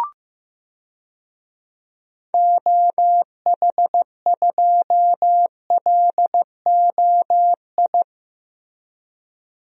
Morse code, heard as a keyed tone, about 15 wpm, 700 Hz. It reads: OH2LOI
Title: DXCC-100 Call Signs - 6 Characters - Repeated 3x - 15wpm